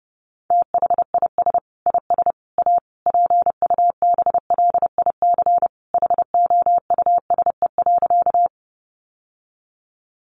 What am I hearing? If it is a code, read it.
Morse code, heard as a keyed tone, about 30 wpm, 700 Hz. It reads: THIS IS A PUBLIC HOUSE.
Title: Sentences from Top 600 Words 30wpm